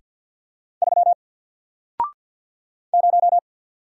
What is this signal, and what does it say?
Morse code, heard as a keyed tone, about 48 wpm, 700 Hz. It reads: 3 0
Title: Single Letter-Number Alt 50wpm